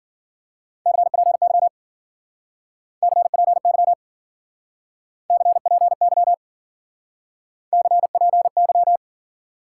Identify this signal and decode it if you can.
Morse code, heard as a keyed tone, about 49 wpm, 700 Hz. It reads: CPY CPY CPY CPY
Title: Speed-Racing Top 90 QSO Elements 40wpm